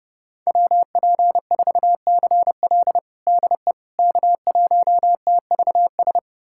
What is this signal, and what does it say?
Morse code, heard as a keyed tone, about 30 wpm, 700 Hz. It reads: WP4CL DE K1TVS
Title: USA Callsign DE Callsign - 5 Characters - 30wpm